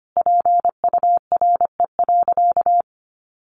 Morse code, heard as a keyed tone, about 25 wpm, 700 Hz. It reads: PURE.
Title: Sentences from Top 1500 Words 25wpm